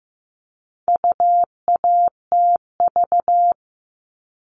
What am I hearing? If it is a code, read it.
Morse code, heard as a keyed tone, about 15 wpm, 700 Hz. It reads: UATV
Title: All Cut Numbers - 4-Digit 15wpm